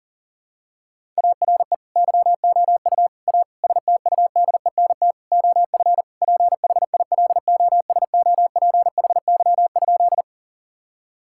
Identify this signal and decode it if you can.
Morse code, heard as a keyed tone, about 40 wpm, 700 Hz. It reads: ARE YOU A STUDENT OF PHILOSOPHY?
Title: Sentences from Top 1000 Words - Review Part 6 - 40wpm